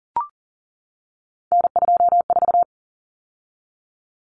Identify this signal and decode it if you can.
Morse code, heard as a keyed tone, about 40 wpm, 700 Hz. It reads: N24
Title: Common Cut Numbers - 3-Digit 40wpm